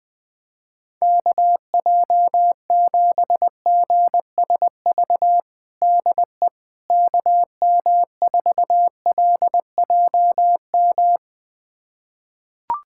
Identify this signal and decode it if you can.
Morse code, heard as a keyed tone, about 20 wpm, 700 Hz. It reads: KJ7GSV DE KM4LJM
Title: USA Callsign DE Callsign - 6 Characters - 20wpm